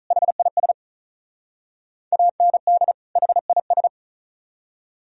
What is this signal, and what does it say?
Morse code, heard as a keyed tone, about 45 wpm, 700 Hz. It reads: HIS AND HIS